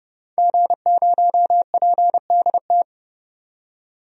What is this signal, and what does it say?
Morse code, heard as a keyed tone, about 32 wpm, 700 Hz. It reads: G0PDT
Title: DXCC-100 Call Signs - 5 Characters - 30wpm